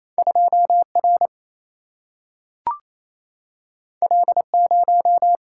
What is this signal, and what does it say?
Morse code, heard as a keyed tone, about 28 wpm, 700 Hz. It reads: 2 R L 0
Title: Single-Letter Single-Number 28wpm